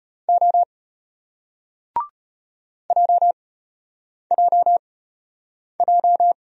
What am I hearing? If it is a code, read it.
Morse code, heard as a keyed tone, about 32 wpm, spaced as for 8 wpm, 700 Hz. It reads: O JJJ